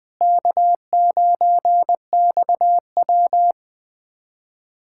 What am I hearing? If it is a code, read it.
Morse code, heard as a keyed tone, about 20 wpm, 700 Hz. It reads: K9XW
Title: USA Callsign DE Callsign - 4 Characters - 20wpm